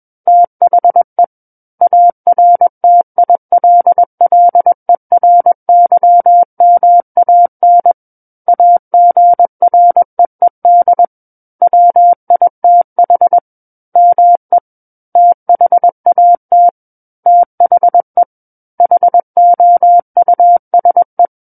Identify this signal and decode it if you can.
Morse code, heard as a keyed tone, about 21 wpm, 700 Hz. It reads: THE ARTILLERYMAN AGREED WITH ME THAT THE HOUSE
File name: Book One / 12-Chapter 12 - 21 WPM